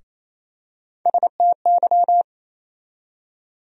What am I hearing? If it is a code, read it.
Morse code, heard as a keyed tone, about 28 wpm, 700 Hz. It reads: STY
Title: USA Call Sign Postfixes - 3 Letters 28wpm